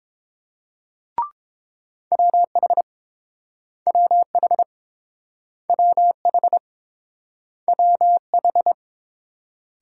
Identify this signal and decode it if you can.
Morse code, heard as a keyed tone, about 29 wpm, 700 Hz. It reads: WH WH WH WH